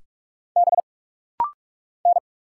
Morse code, heard as a keyed tone, about 44 wpm, 700 Hz. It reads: B N
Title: Single Letter-Number - Rapid-Fire 45wpm